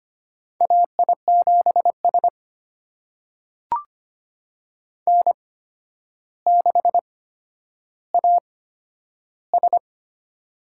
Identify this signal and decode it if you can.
Morse code, heard as a keyed tone, about 25 wpm, spaced as for 11 wpm, 700 Hz. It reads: AI7S N6AS
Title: USA Call Signs - 4 Characters - 8x Character Spacing - 25wpm